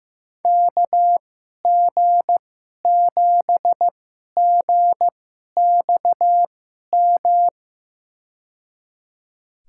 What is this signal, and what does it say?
Morse code, heard as a keyed tone, about 15 wpm, spaced as for 11 wpm, 700 Hz. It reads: KG7GXM
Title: USA Call Signs - 6 Characters - 2x Character Spacing - 15wpm